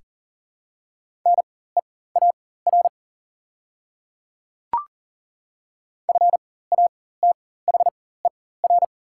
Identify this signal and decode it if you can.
Morse code, heard as a keyed tone, about 40 wpm, spaced as for 19 wpm, 700 Hz. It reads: NEAR FATHER
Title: 4x Character Spacing - Top 500 Words - 40wpm